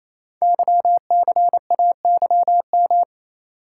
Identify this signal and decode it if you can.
Morse code, heard as a keyed tone, about 27 wpm, 700 Hz. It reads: YCAYM
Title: Code Groups - 5 Letters - 28wpm